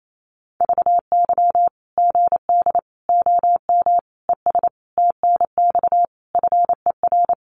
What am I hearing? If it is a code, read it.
Morse code, heard as a keyed tone, about 28 wpm, 700 Hz. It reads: VY GD OM ES TNX FER
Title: Common QSO Phrases 28wpm